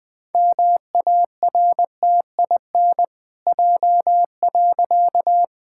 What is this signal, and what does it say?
Morse code, heard as a keyed tone, about 20 wpm, 700 Hz. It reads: MARTIN J.
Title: Pangram Challenge v2 - 20wpm